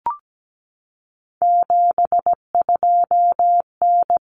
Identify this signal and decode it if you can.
Morse code, heard as a keyed tone, about 17 wpm, 700 Hz. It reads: 72N
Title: Common Cut Numbers - 3-Digit 17wpm